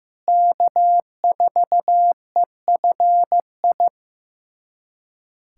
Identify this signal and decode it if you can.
Morse code, heard as a keyed tone, about 15 wpm, 700 Hz. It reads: K4EFI
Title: USA Callsign DE Callsign - 5 Characters - 15wpm